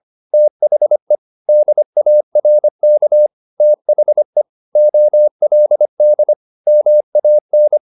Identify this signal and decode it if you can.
Morse code, heard as a keyed tone, about 25 wpm, 600 Hz. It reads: THE DARK THE OLD MAN
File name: the old man and the sea part 1 25wpm